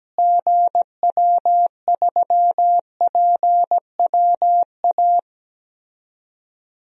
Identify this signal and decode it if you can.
Morse code, heard as a keyed tone, about 17 wpm, 700 Hz. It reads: GW3PWA
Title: DXCC-100 Call Signs - 6 Characters - Repeated 3x - 17wpm